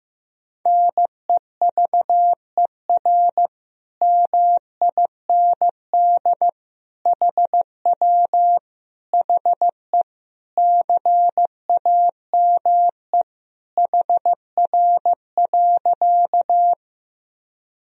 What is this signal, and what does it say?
Morse code, heard as a keyed tone, about 16 wpm, 700 Hz. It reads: NEVER MIND HW HE CAME HR.
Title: Sentences from Top 500 Words and Abbreviations 15wpm